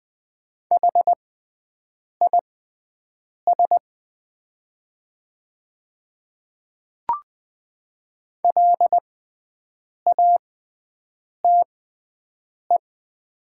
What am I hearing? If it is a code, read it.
Morse code, heard as a keyed tone, about 20 wpm, spaced as for 7 wpm, 700 Hz. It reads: HIS LATE
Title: 6x Character Spacing - Top 300 Words - 20wpm